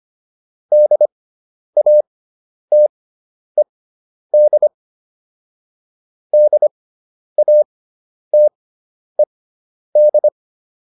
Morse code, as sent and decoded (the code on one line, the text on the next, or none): -.. .- - . -.. / -.. .- - . -..
DATED DATED